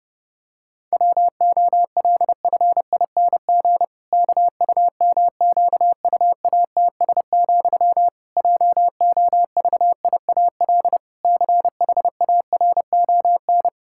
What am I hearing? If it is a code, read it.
Morse code, heard as a keyed tone, about 30 wpm, 700 Hz. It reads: WOLFING KUMQUATS, JOVIAL CHARON
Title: Pangram Challenge v2 - 30wpm